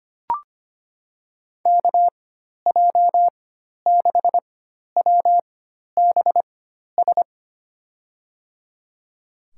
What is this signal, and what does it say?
Morse code, heard as a keyed tone, about 25 wpm, spaced as for 12 wpm, 700 Hz. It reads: KJ6WBS
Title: USA Call Signs - 6 Characters - 4x Character Spacing - 25wpm